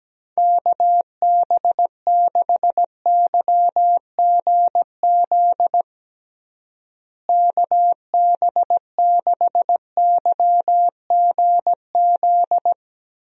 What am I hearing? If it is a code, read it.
Morse code, heard as a keyed tone, about 17 wpm, 700 Hz. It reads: KB6YGZ KB6YGZ
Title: USA Call Signs - 6 Characters - Repeated 2x - 17wpm